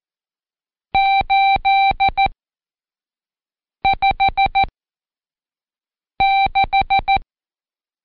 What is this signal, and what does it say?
Morse code, heard as a keyed tone, about 14 wpm, spaced as for 5 wpm, 780 Hz. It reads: 856
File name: Code Quick / AudioDownld1 / Seg82-Day9-Intro 8